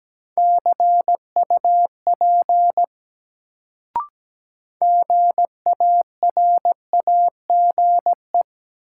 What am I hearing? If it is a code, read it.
Morse code, heard as a keyed tone, about 17 wpm, 700 Hz. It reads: CUP GARAGE
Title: Sentences from Top 1600 Words - New Word Drill - 17wpm